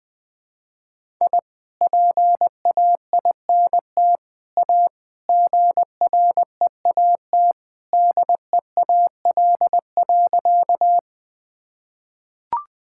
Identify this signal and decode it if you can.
Morse code, heard as a keyed tone, about 21 wpm, 700 Hz. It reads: I PAINT A GREAT DEAL.
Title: Sentences from Top 1000 Words - Review Part 6 - 20wpm